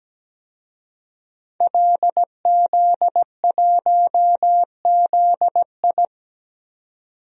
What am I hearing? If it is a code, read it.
Morse code, heard as a keyed tone, about 17 wpm, 700 Hz. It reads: LZ1ZI
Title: DXCC-100 Call Signs - 5 Characters - 17wpm